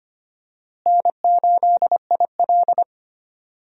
Morse code, heard as a keyed tone, about 25 wpm, 700 Hz. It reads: N8IL
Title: USA Call Signs - 4 Characters - Repeated 2x - 25wpm